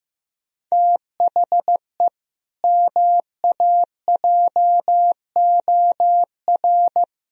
Morse code, heard as a keyed tone, about 15 wpm, 700 Hz. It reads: THE MAJOR
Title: Sentences from Top 1000 Words - Review Part 4 - 15wpm